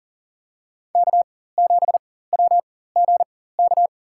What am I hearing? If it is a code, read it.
Morse code, heard as a keyed tone, about 39 wpm, spaced as for 18 wpm, 700 Hz. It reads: K7WGK
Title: USA Call Signs - 5 Characters - 4x Character Spacing - 40wpm